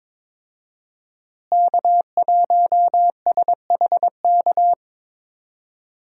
Morse code, heard as a keyed tone, about 22 wpm, 700 Hz. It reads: K1SHK
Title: USA Call Signs - 5 Characters - Repeated 2x - 22wpm